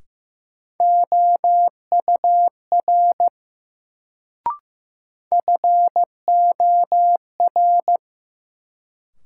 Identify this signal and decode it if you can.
Morse code, heard as a keyed tone, about 15 wpm, 700 Hz. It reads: OUR FOR
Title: Sets of 2 Words - New Word Drill - 15wpm